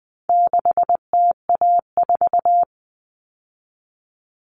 Morse code, heard as a keyed tone, about 20 wpm, 700 Hz. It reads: 6TA4
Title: All Cut Numbers - 4-Digit 20wpm